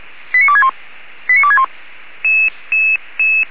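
Five-tone selective call (ZVEI), heard as quick runs of five tones, tones of about 70 ms. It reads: 89261 89261
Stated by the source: Received on an SDR; ZVEI five-tone call.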